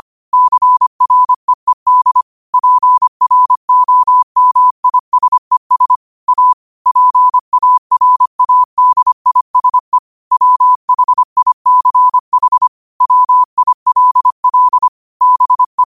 Morse code, heard as a keyed tone, about 25 wpm, 1000 Hz. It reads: CREED PROMISES A PARADISE WHICH WILL BE
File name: qotdfiles / 25wpm